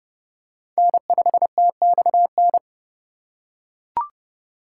Morse code, heard as a keyed tone, about 30 wpm, 700 Hz. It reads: N5TXN